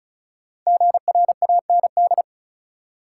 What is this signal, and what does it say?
Morse code, heard as a keyed tone, about 36 wpm, 700 Hz. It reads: GRAND
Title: Five Letter Words - 35wpm